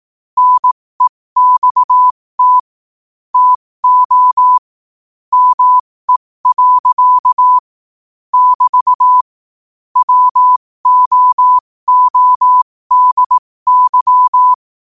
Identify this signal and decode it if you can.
Morse code, heard as a keyed tone, about 18 wpm, spaced as for 15 wpm, 1000 Hz. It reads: NEXT TO ME. <BT> WOODY
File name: qotdfiles / 15wpm